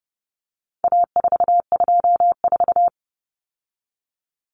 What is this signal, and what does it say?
Morse code, heard as a keyed tone, about 30 wpm, 700 Hz. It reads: A424
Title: Common Cut Numbers - 4-Digit 30wpm